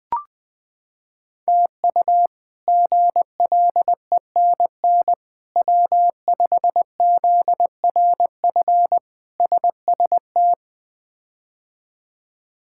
Morse code, heard as a keyed tone, about 20 wpm, 700 Hz. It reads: TU GLENN W5ZRF SST
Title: K1USN SST QSO Practice - 20wpm